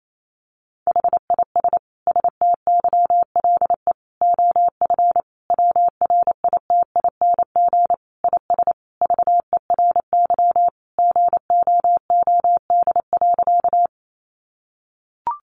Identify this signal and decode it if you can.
Morse code, heard as a keyed tone, about 28 wpm, 700 Hz. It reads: HIS STYLE OF WRITING IS VERY GOOD.
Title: Sentences from Top 800 Words 28wpm